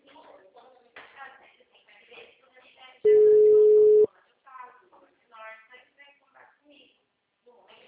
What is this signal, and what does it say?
Ringback tone (Europe, 425 Hz), with speech.
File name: out- -2008- - - .164